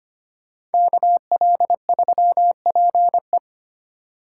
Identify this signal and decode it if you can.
Morse code, heard as a keyed tone, about 33 wpm, 700 Hz. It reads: KL3PE